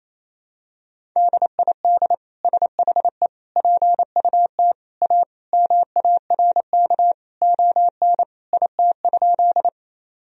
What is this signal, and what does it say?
Morse code, heard as a keyed tone, about 28 wpm, 700 Hz. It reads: DID SHE PUT A MARK ON IT?
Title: Sentences from Top 1000 Words - Review Part 1 - 28wpm